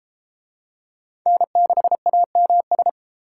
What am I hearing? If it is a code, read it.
Morse code, heard as a keyed tone, about 31 wpm, 700 Hz. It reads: N6AMS